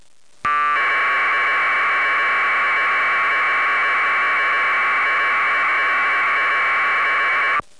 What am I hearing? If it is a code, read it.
Packet radio (AX.25, as in APRS).
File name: packet / pkt1200.wav / pkt1200